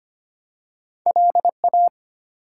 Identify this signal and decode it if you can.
Morse code, heard as a keyed tone, about 25 wpm, 700 Hz. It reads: LA
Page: morse-code-ninja-practice-25wpm